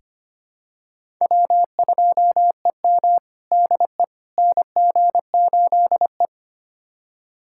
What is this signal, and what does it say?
Morse code, heard as a keyed tone, about 25 wpm, 700 Hz. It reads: W2EM DE NG8E
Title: USA Callsign DE Callsign - 4 Characters - 25wpm